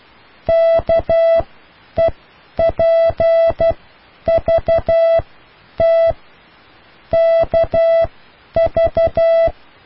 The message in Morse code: KEPVT KV